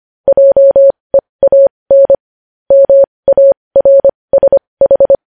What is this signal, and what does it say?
Morse code, heard as a keyed tone, about 25 wpm, spaced as for 20 wpm, 550 Hz. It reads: JEAN MARSH
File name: podcast / News headlines-1x-25x20